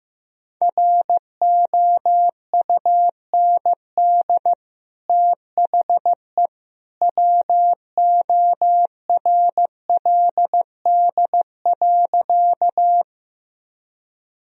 Morse code, heard as a keyed tone, about 15 wpm, 700 Hz. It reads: ROUND THE WORLD.
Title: Sentences from Top 1000 Words - Review Part 1 - 15wpm